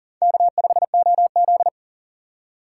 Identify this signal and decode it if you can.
Morse code, heard as a keyed tone, about 53 wpm, spaced as for 47 wpm, 700 Hz. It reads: K5OZ
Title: Speed Racing - USA 1x2 Call Signs 40wpm